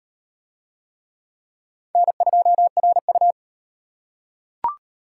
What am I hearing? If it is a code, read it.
Morse code, heard as a keyed tone, about 35 wpm, 700 Hz. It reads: N2RU